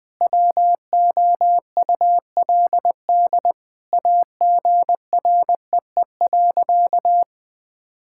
Morse code, heard as a keyed tone, about 20 wpm, 700 Hz. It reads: WOULD AGREE.
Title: Sentences with 2-Word Phrases - 20wpm